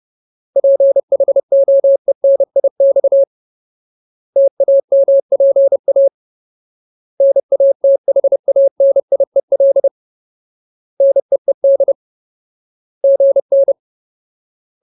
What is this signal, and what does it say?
Morse code, heard as a keyed tone, about 30 wpm, 550 Hz. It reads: PHOENIX TAMPA NATHANIEL NEED GN
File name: Ultimate Word List / 30WPM Ultimate